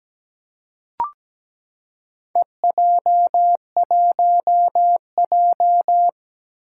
Morse code, heard as a keyed tone, about 17 wpm, 700 Hz. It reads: EJ1J
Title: Four Challenging Letter-Number Combinations 17wpm